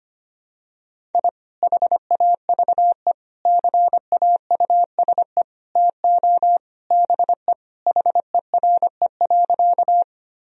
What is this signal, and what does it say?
Morse code, heard as a keyed tone, about 25 wpm, 700 Hz. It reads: I HAVE CAUSE TO BE HERE.
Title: Sentences from Top 1000 Words - Review Part 1 - 25wpm